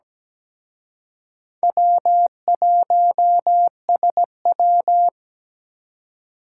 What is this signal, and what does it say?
Morse code, heard as a keyed tone, about 20 wpm, 700 Hz. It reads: W1SW
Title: Speed Racing - USA 1x2 Call Signs 17wpm